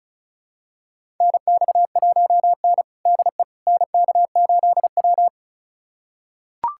Morse code, heard as a keyed tone, about 34 wpm, 700 Hz. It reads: NX1N DE NK8W
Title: USA Callsign DE Callsign - 4 Characters - 35wpm